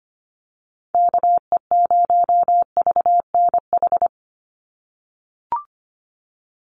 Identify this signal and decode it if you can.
Morse code, heard as a keyed tone, about 25 wpm, 700 Hz. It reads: KE0VNH
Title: USA Call Signs 25wpm